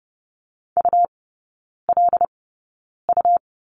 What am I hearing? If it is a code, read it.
Morse code, heard as a keyed tone, about 30 wpm, spaced as for 9 wpm, 700 Hz. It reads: ULU
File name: Lesson 18e - Letter m - Three Characters 30wpm